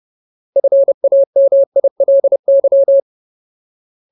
Morse code, heard as a keyed tone, about 30 wpm, 550 Hz. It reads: FAMILY
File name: Ultimate Word List / 30WPM Ultimate